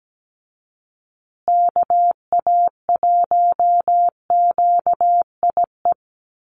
Morse code, heard as a keyed tone, about 17 wpm, 700 Hz. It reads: KA1QIE
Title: USA Call Signs Repeated 3x 17wpm